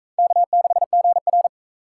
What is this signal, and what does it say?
Morse code, heard as a keyed tone, about 41 wpm, 700 Hz. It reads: K6GR